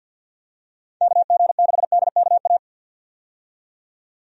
Morse code, heard as a keyed tone, about 47 wpm, 700 Hz. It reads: KG6DKA
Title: USA Call Signs - 6 Characters - 50wpm